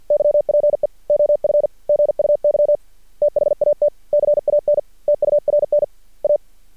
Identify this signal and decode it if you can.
Morse code, heard as a keyed tone, about 51 wpm, 590 Hz. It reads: YPE OF GUY THAT CAN TURN A
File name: archived / CRSnet